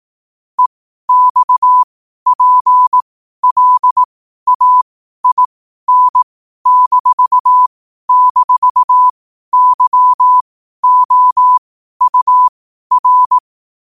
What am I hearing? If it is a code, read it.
Morse code, heard as a keyed tone, about 18 wpm, spaced as for 12 wpm, 1000 Hz. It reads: EXPLAIN--YOUR
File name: qotdfiles / 12wpm